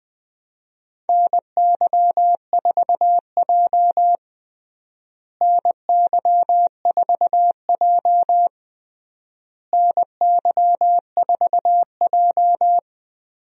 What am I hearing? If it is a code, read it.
Morse code, heard as a keyed tone, about 21 wpm, 700 Hz. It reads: NY4J NY4J NY4J
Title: USA Call Signs - 4 Characters - Repeated 3x - 20wpm